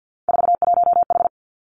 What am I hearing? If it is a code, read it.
Morse code, heard as a keyed tone, about 50 wpm, 700 Hz. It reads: 41H
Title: Three Challenging Letter-Number Combinations 50wpm